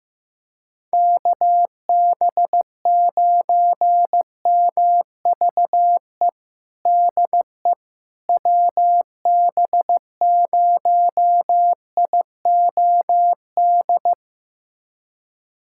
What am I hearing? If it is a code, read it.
Morse code, heard as a keyed tone, about 15 wpm, 700 Hz. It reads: KB9MVE DE WB0IOD
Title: USA Callsign DE Callsign 15wpm